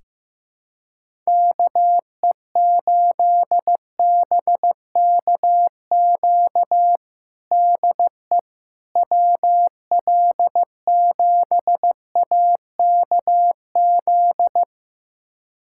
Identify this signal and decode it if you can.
Morse code, heard as a keyed tone, about 15 wpm, 700 Hz. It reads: KE8BKQ DE WL7AKZ
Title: USA Callsign DE Callsign - 6 Characters - 15wpm